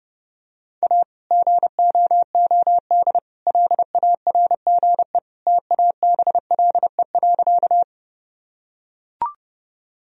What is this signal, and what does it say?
Morse code, heard as a keyed tone, about 30 wpm, 700 Hz. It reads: A GOOD LARGE TABLE.
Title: Sentences from Top 1000 Words - Review Part 2 - 30wpm